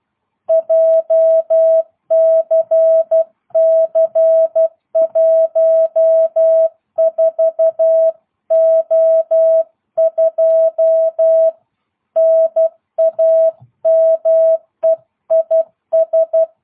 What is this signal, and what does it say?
Morse code, heard as a keyed tone, about 12 wpm, 660 Hz. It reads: JCC14O2 NAME IS
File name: jh8cbh / 2022cw